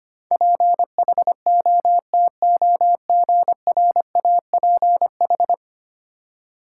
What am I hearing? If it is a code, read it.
Morse code, heard as a keyed tone, about 25 wpm, 700 Hz. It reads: PHOTOGRAPH
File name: 10 Letter Words - First 3 Letters Given - 25wpm